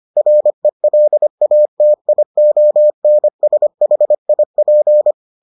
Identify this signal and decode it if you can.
Morse code, heard as a keyed tone, about 25 wpm, 600 Hz. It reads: RELATIONSHIP